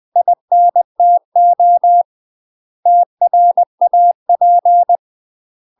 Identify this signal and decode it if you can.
Morse code, heard as a keyed tone, about 20 wpm, 700 Hz. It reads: INTO TRAP
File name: UK MASTER QSOs TRAINING / 20WPM / UK MASTER QSOs 0035